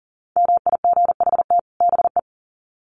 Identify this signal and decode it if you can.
Morse code, heard as a keyed tone, about 40 wpm, 700 Hz. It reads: MIGHT BE
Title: Sets of 2 Words 40wpm